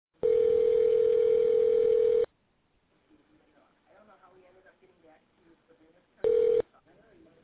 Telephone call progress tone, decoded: ringback tone

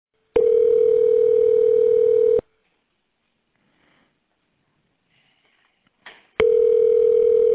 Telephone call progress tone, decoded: ringback tone

